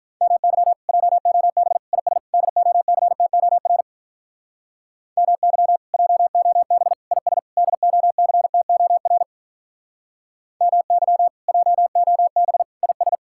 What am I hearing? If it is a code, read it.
Morse code, heard as a keyed tone, about 44 wpm, 700 Hz. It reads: MY JOB IS DOCTOR MY JOB IS DOCTOR MY JOB IS